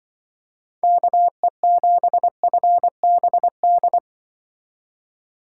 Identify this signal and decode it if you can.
Morse code, heard as a keyed tone, about 24 wpm, 700 Hz. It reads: KE7FBD
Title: Speed Racing - USA 2x3 Call Signs 20wpm